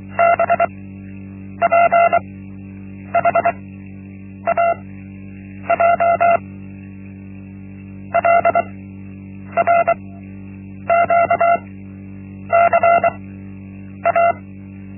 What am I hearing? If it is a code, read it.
Morse code, heard as a keyed tone, about 23 wpm, spaced as for 8 wpm, 680 Hz. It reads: BPHAJ LRQCA
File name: Linksmakalnis-1